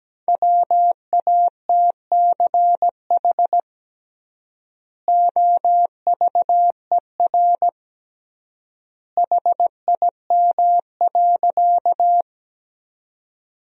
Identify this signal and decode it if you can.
Morse code, heard as a keyed tone, about 17 wpm, 700 Hz. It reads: WATCH OVER HIM.
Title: 3x Word-Spacing - Sentences from Top 300 Words - 17wpm